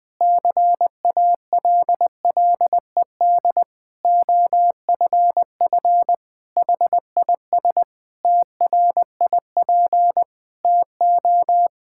Morse code, heard as a keyed tone, about 20 wpm, 700 Hz. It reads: CALLED OFF HIS TRIP TO
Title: Pangram Challenge v2 - 20wpm